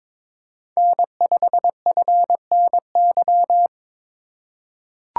Morse code, heard as a keyed tone, about 21 wpm, 700 Hz. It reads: N5FNY
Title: USA Call Signs - 5 Characters - 4x Character Spacing - 22wpm